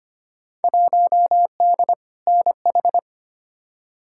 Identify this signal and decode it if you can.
Morse code, heard as a keyed tone, about 25 wpm, 700 Hz. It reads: 1D NH
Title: Field Day Exchanges - 25wpm